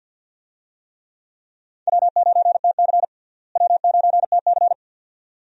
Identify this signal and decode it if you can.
Morse code, heard as a keyed tone, about 47 wpm, 700 Hz. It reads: W9TC W9TC
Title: USA Call Signs - 4 Characters - Repeated 2x - 50wpm